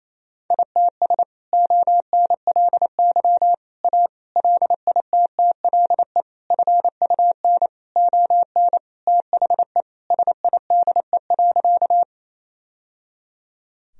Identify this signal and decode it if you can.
Morse code, heard as a keyed tone, about 27 wpm, 700 Hz. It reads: ITS ONLY A LITTLE FUN ON THE SIDE.
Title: Sentences from Top 1000 Words - Review Part 4 - 28wpm